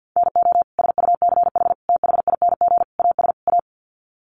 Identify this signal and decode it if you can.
Morse code, heard as a keyed tone, about 50 wpm, 700 Hz. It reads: NO SUCH THING AS A
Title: Sets of 5 Words Encore 50wpm